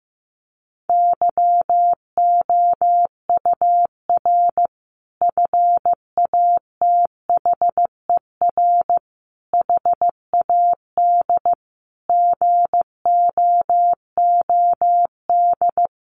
Morse code, heard as a keyed tone, about 15 wpm, 700 Hz. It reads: YOUR FATHER HAD GOOD
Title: Sentences from Top 1100 Words 15wpm